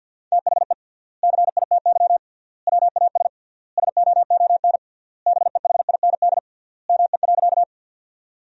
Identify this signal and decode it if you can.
Morse code, heard as a keyed tone, about 50 wpm, 700 Hz. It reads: THE CITY WAS SOON BEHIND ME.
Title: 3x Word-Spacing - Sentences from Top 500 Words - 50wpm